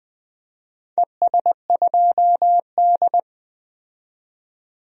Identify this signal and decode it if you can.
Morse code, heard as a keyed tone, about 20 wpm, 700 Hz. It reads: ES2D
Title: DXCC-100 Call Signs - 4 Characters - 20wpm